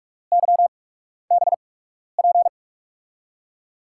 Morse code, heard as a keyed tone, about 44 wpm, spaced as for 12 wpm, 700 Hz. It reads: YBP
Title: USA Call Signs - 6 Characters - 8x Character Spacing - 45wpm